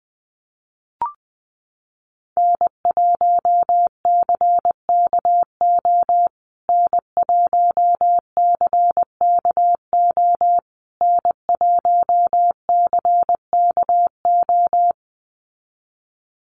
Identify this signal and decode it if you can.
Morse code, heard as a keyed tone, about 20 wpm, 700 Hz. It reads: N1CKO N1CKO N1CKO
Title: USA Call Signs Repeated 3x 20wpm